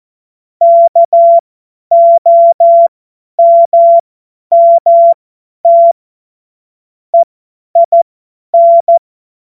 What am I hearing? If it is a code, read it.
Morse code, heard as a keyed tone, about 14 wpm, spaced as for 10 wpm, 680 Hz. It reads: KOMMT EIN
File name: Morsekurs / Witze / Witz 27. April 2021, 50 F70